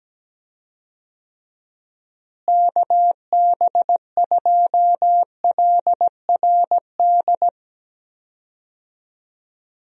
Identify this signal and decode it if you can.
Morse code, heard as a keyed tone, about 17 wpm, 700 Hz. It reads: KB2LRD
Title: USA Call Signs - 6 Characters - Repeated 3x - 17wpm